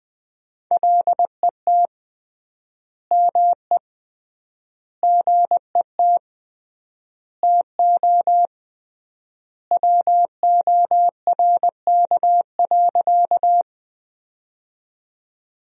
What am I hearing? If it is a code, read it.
Morse code, heard as a keyed tone, about 20 wpm, 700 Hz. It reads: LET ME GET TO WORK.
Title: 3x Word-Spacing - Sentences from Top 300 Words - 20wpm